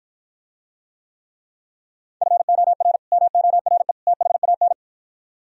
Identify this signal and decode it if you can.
Morse code, heard as a keyed tone, about 52 wpm, 700 Hz. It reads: FOR MORE THAN